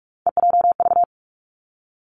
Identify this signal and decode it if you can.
Morse code, heard as a keyed tone, about 46 wpm, 700 Hz. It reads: EJV
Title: Three Challenging Letter-Number Combinations 45wpm